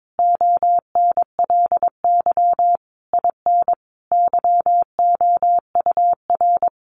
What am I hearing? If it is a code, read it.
Morse code, heard as a keyed tone, about 22 wpm, 700 Hz. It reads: ONLY IN YOUR
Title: Sentences from Top 1300 Words 22wpm